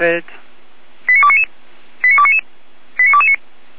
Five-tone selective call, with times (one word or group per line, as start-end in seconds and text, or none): 1.1-1.4 89299
2.0-2.4 89299
3.0-3.4 89229